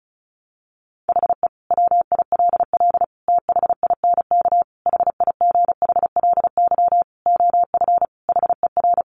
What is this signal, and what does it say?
Morse code, heard as a keyed tone, about 34 wpm, 700 Hz. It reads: HE WILL THINK HIGHLY OF HER
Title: Sentences from Top 1100 Words 35wpm